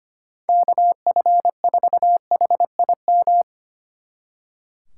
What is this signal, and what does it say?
Morse code, heard as a keyed tone, about 32 wpm, 700 Hz. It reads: KF4HIM